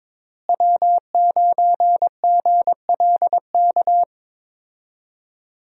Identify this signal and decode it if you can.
Morse code, heard as a keyed tone, about 22 wpm, 700 Hz. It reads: W9GLK
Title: USA Call Signs - 5 Characters - Repeated 2x - 22wpm